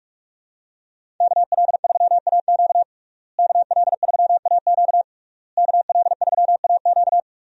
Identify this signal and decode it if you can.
Morse code, heard as a keyed tone, about 46 wpm, 700 Hz. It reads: KL3AQ KL3AQ KL3AQ
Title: USA Call Signs - 5 Characters - Repeated 3x - 45wpm